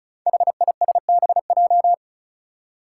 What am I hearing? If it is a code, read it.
Morse code, heard as a keyed tone, about 36 wpm, 700 Hz. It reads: HISBJ
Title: Five Challenging Letter-Number Combinations 35wpm